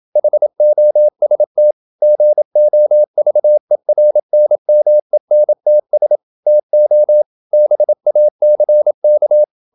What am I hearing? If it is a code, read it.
Morse code, heard as a keyed tone, about 27 wpm, 600 Hz. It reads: HOST GOVERNMENTS TO BACK
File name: Never 0027wpm